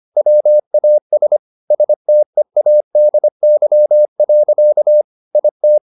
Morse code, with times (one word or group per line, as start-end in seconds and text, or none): 0.2-1.4 WAS
1.7-5.0 STEADY.
5.3-5.8 IT